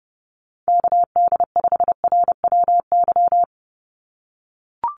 Morse code, heard as a keyed tone, about 28 wpm, 700 Hz. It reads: KD5RWY
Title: USA Call Signs 30wpm